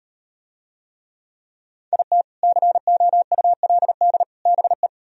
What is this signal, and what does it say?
Morse code, heard as a keyed tone, about 38 wpm, 700 Hz. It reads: IT COULD BE